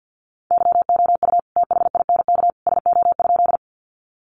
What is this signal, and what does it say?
Morse code, heard as a keyed tone, about 50 wpm, 700 Hz. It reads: YOU THINK SO?
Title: Sentences from Top 200 Words 50wpm